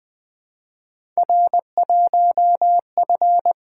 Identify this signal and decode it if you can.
Morse code, heard as a keyed tone, about 20 wpm, 700 Hz. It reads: R1F
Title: Three Letter-Number Combinations - 20wpm